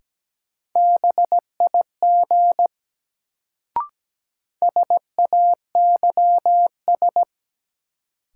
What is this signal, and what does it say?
Morse code, heard as a keyed tone, about 17 wpm, 700 Hz. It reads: BIG SAYS
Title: Sets of 2 Words - New Word Drill - 17wpm